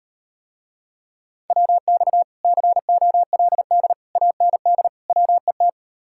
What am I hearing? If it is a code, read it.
Morse code, heard as a keyed tone, about 38 wpm, 700 Hz. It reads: WX COLD AND WET